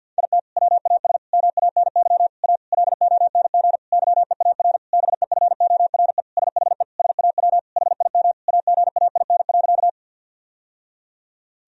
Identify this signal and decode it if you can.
Morse code, heard as a keyed tone, about 52 wpm, 700 Hz. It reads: IT WAS MANY A LONG YEAR BEFORE SHE SAW HIM AGAIN.
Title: Sentences from Top 1000 Words - Review Part 1 - 50wpm